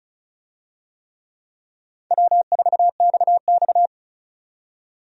Morse code, heard as a keyed tone, about 36 wpm, 700 Hz. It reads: W4XX
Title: USA Call Signs - 4 Characters - 35wpm